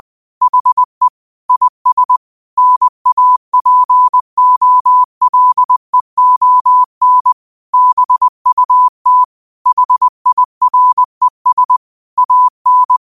Morse code, heard as a keyed tone, about 20 wpm, 1000 Hz. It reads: HE IS NAPOLEON BUT HIRES AN
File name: qotdfiles / 20wpm